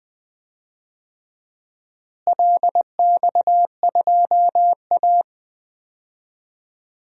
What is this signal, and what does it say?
Morse code, heard as a keyed tone, about 20 wpm, 700 Hz. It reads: LX2A
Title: DXCC-100 Call Signs - 4 Characters - Repeated 2x - 20wpm